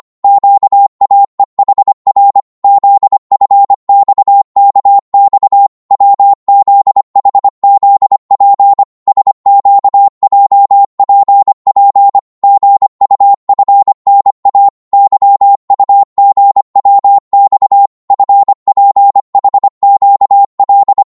Morse code, heard as a keyed tone, about 25 wpm, 800 Hz. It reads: QAEHR ZFXKX WZHZP SQJPP GUFNA YUGWX FPHQL
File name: Morse / ltr-25wpm